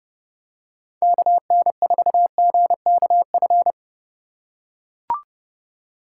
Morse code, heard as a keyed tone, about 29 wpm, 700 Hz. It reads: KN4GKF